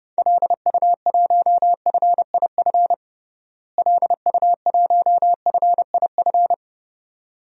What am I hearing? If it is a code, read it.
Morse code, heard as a keyed tone, about 30 wpm, 700 Hz. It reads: LU1FIF LU1FIF
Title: DXCC-100 Call Signs - 6 Characters - Repeated 2x - 30wpm